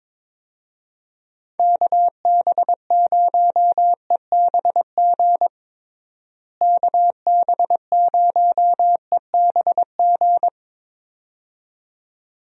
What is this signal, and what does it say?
Morse code, heard as a keyed tone, about 21 wpm, 700 Hz. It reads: KB0EBG KB0EBG
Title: USA Call Signs - 6 Characters - Repeated 2x - 22wpm